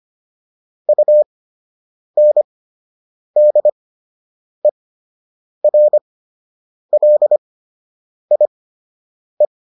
Morse code, heard as a keyed tone, about 25 wpm, spaced as for 8 wpm, 600 Hz. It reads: UNDERLIE